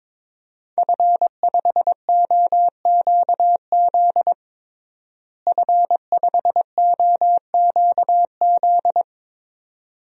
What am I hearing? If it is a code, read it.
Morse code, heard as a keyed tone, about 22 wpm, 700 Hz. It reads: F5OQZ F5OQZ
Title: DXCC-100 Call Signs - 5 Characters - Repeated 2x - 22wpm